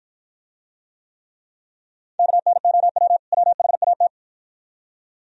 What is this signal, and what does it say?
Morse code, heard as a keyed tone, about 53 wpm, 700 Hz. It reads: KNOW WHAT